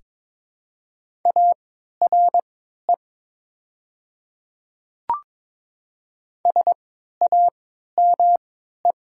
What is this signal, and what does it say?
Morse code, heard as a keyed tone, about 22 wpm, spaced as for 13 wpm, 700 Hz. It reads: ARE SAME
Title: 3x Character Spacing - Top 300 Words - 22wpm